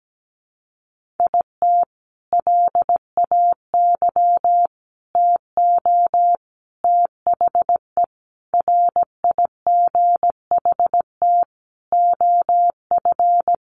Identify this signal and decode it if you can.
Morse code, heard as a keyed tone, about 17 wpm, 700 Hz. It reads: IT LAY TO THE RIGHT OF
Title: Sentences from Top 1400 Words 17wpm